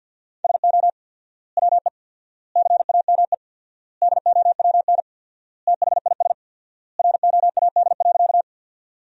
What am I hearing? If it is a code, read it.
Morse code, heard as a keyed tone, about 50 wpm, 700 Hz. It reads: SO WE CAME DOWN THIS ROAD.
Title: 4x Word-Spacing - Sentences from Top 400 Words - 50wpm